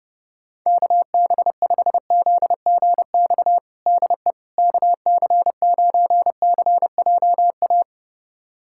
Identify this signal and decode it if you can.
Morse code, heard as a keyed tone, about 30 wpm, 700 Hz. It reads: KB5ZGX DE KC9CJA
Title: USA Callsign DE Callsign - 6 Characters - 30wpm